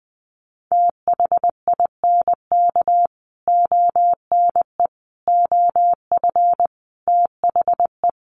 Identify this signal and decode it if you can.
Morse code, heard as a keyed tone, about 20 wpm, 700 Hz. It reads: THINK ONE OF THE
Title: Sets of 5 Words Encore2 20wpm